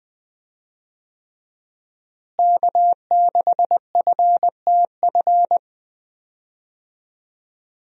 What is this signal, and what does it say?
Morse code, heard as a keyed tone, about 19 wpm, 700 Hz. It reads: K6FTF
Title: USA Call Signs - 5 Characters - Repeated 3x - 20wpm